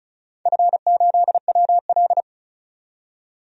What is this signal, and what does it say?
Morse code, heard as a keyed tone, about 36 wpm, 700 Hz. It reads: F8WL
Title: DXCC-100 Call Signs - 4 Characters - 35wpm